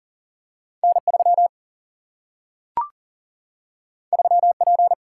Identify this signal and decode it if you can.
Morse code, heard as a keyed tone, about 40 wpm, 700 Hz. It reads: N3 3P
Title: Single-Letter Single-Number 40wpm